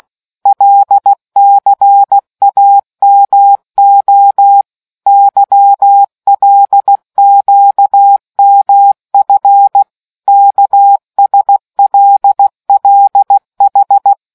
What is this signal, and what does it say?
Morse code, heard as a keyed tone, about 16 wpm, 800 Hz. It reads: LCAMO YLQMF KSLLH